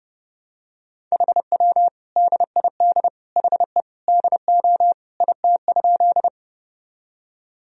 Morse code, heard as a keyed tone, about 29 wpm, 700 Hz. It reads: HW DID HE DO IT?
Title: Sentences from Top 100 Words and Abbreviations 30wpm